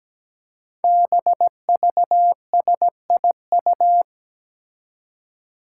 Morse code, heard as a keyed tone, about 17 wpm, 700 Hz. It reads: BVSIU
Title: Five Challenging Letter-Number Combinations 17wpm